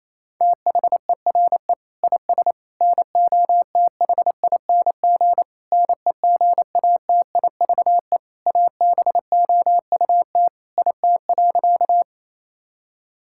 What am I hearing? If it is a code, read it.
Morse code, heard as a keyed tone, about 28 wpm, 700 Hz. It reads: THERE IS NOTHING NEGATIVE ABOUT IT.
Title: Sentences from Top 1000 Words - Review Part 6 - 28wpm